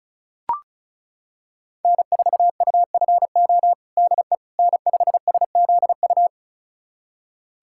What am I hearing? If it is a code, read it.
Morse code, heard as a keyed tone, about 34 wpm, 700 Hz. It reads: N4UFO DE N5SZU
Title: USA Callsign DE Callsign - 5 Characters - 35wpm